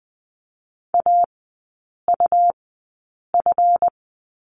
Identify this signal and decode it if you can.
Morse code, heard as a keyed tone, about 20 wpm, spaced as for 8 wpm, 700 Hz. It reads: AUF
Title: 138 Lesson 23e - Letter f - Three Characters 20wpm